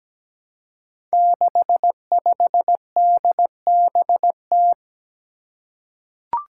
Morse code, heard as a keyed tone, about 17 wpm, 700 Hz. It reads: 65DBT
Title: Five Challenging Letter-Number Combinations 17wpm